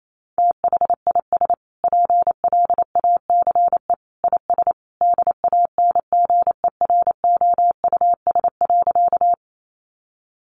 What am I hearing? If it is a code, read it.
Morse code, heard as a keyed tone, about 28 wpm, 700 Hz. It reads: THIS PLACE IS DANGEROUS.
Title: Sentences from Top 1200 Words 28wpm